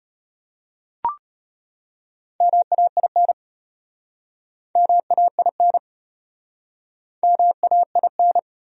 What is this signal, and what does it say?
Morse code, heard as a keyed tone, about 38 wpm, 700 Hz. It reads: MAIN MAIN MAIN